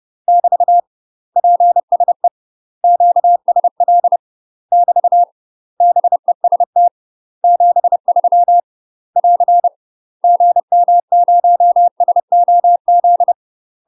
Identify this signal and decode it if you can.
Morse code, heard as a keyed tone, about 30 wpm, 700 Hz. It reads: <BT> PSE QSL <BT> BEST 73 <AR> GM0SOZ
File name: UK MASTER QSOs TRAINING / 30WPM / UK MASTER QSOs 0046